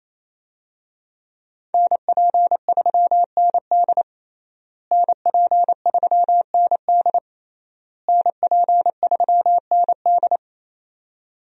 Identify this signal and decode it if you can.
Morse code, heard as a keyed tone, about 28 wpm, 700 Hz. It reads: NP3ND NP3ND NP3ND
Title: USA Call Signs - 5 Characters - Repeated 3x - 28wpm